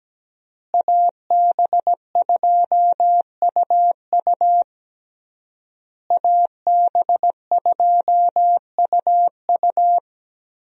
Morse code, heard as a keyed tone, about 17 wpm, 700 Hz. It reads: AB2UU AB2UU
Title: USA Call Signs - 5 Characters - Repeated 3x - 17wpm